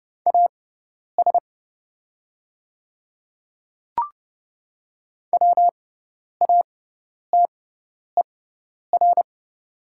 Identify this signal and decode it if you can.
Morse code, heard as a keyed tone, about 29 wpm, spaced as for 10 wpm, 700 Hz. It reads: AS WATER
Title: 6x Character Spacing - Top 100 Words - 30wpm